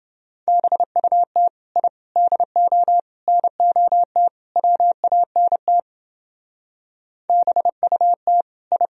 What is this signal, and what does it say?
Morse code, heard as a keyed tone, about 29 wpm, 700 Hz. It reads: BUT I DO NOT WANT BUT I